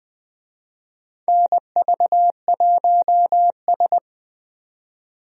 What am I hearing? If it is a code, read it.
Morse code, heard as a keyed tone, about 20 wpm, 700 Hz. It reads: NV1S
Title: USA Call Signs - 4 Characters - 20wpm